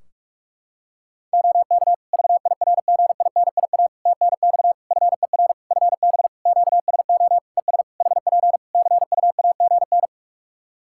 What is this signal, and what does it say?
Morse code, heard as a keyed tone, about 44 wpm, 700 Hz. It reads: OK VIRGINIA TNX FER FB QSO ES HP CUAGN
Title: Practice QSO - Exchange 4 - Ending - 45wpm